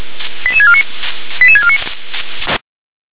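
Five-tone selective call (ZVEI), heard as quick runs of five tones, tones of about 70 ms. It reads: 88649 88649
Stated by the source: ZVEI five-tone selective call received on an SDR.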